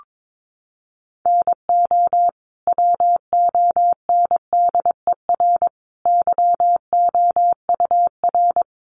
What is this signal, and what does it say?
Morse code, heard as a keyed tone, about 22 wpm, 700 Hz. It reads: NO WONDER YOUR
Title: Sentences from Top 1100 Words 22wpm